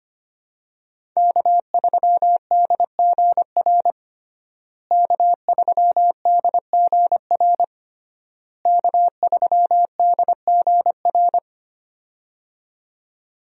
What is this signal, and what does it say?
Morse code, heard as a keyed tone, about 25 wpm, 700 Hz. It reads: K3DGR K3DGR K3DGR
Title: USA Call Signs - 5 Characters - Repeated 3x - 25wpm